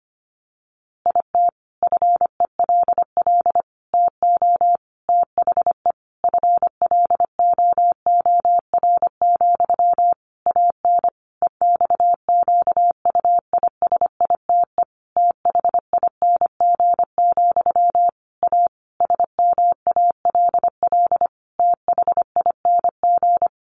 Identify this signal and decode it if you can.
Morse code, heard as a keyed tone, about 25 wpm, 700 Hz. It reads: IT FELL TO THE FLOOR, AN EXQUISITE THING, A SMALL THING
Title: A Sound Of Thunder 25wpm